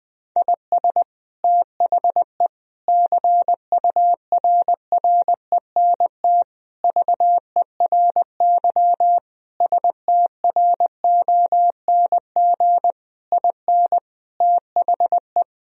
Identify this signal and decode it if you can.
Morse code, heard as a keyed tone, about 21 wpm, 700 Hz. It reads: IS THE CURRENT VERY STRONG IN THE
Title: Sentences from Top 1000 Words - Review Part 4 - 20wpm